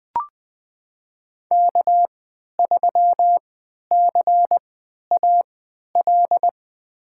Morse code, heard as a keyed tone, about 21 wpm, spaced as for 11 wpm, 700 Hz. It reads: K3CAL
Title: USA Call Signs - 3x Character Spacing - 20wpm